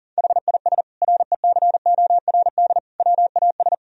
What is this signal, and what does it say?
Morse code, heard as a keyed tone, about 40 wpm, 700 Hz. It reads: HIS RECORD WAS
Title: Sentences from Top 1000 Words - Review Part 3 - 40wpm